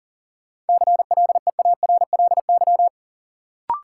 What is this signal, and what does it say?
Morse code, heard as a keyed tone, about 38 wpm, 700 Hz. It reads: CLEARLY
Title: Sentences from Top 1300 Words - New Word Drill - 40wpm